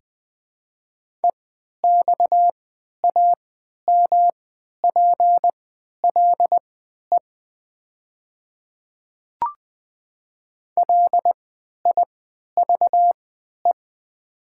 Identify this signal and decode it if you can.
Morse code, heard as a keyed tone, about 20 wpm, spaced as for 11 wpm, 700 Hz. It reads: EXAMPLE LIVE
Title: 3x Character Spacing - Top 300 Words - 20wpm